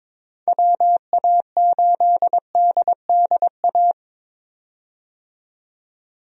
Morse code, heard as a keyed tone, about 22 wpm, 700 Hz. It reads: WA8DDA
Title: USA Call Signs - 6 Characters - 22wpm